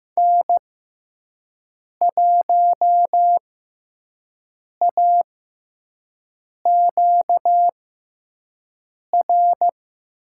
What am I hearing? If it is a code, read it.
Morse code, heard as a keyed tone, about 15 wpm, spaced as for 5 wpm, 700 Hz. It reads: N1AQR
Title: USA Call Signs - 5 Characters - 6x Character Spacing - 15wpm